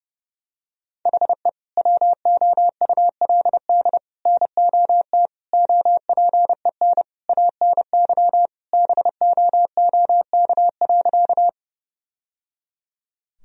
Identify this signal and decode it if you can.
Morse code, heard as a keyed tone, about 30 wpm, 700 Hz. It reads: HE WOULD NOT OPEN ANY BOOK.
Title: Sentences from Top 1000 Words - Review Part 1 - 30wpm